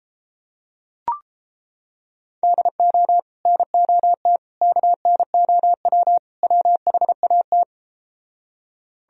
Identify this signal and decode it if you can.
Morse code, heard as a keyed tone, about 33 wpm, 700 Hz. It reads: DO NOT KNOW WHAT